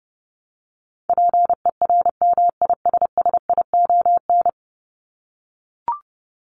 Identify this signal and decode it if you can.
Morse code, heard as a keyed tone, about 30 wpm, 700 Hz. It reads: PERMISSION
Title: Top 1501-1750 Words 30wpm